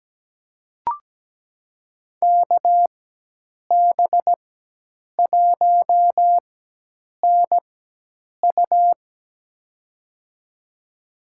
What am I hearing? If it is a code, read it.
Morse code, heard as a keyed tone, about 17 wpm, spaced as for 8 wpm, 700 Hz. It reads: KB1NU
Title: USA Call Signs - 5 Characters - 4x Character Spacing - 17wpm